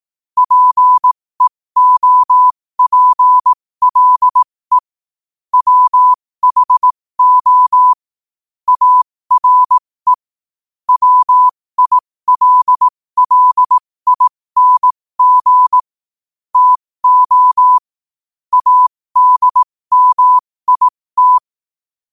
Morse code, heard as a keyed tone, about 18 wpm, spaced as for 15 wpm, 1000 Hz. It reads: PEOPLE WHO ARE WILLING TO ADMIT
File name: qotdfiles / 15wpm